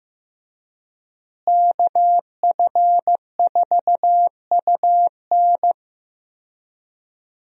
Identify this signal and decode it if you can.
Morse code, heard as a keyed tone, about 15 wpm, 700 Hz. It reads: KF4UN
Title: USA Call Signs - 5 Characters - Repeated 3x - 15wpm